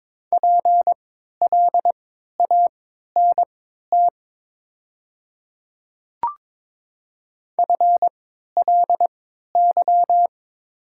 Morse code, heard as a keyed tone, about 22 wpm, spaced as for 12 wpm, 700 Hz. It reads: PLANT FLY
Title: 3x Character Spacing - Top 500 Words - 22wpm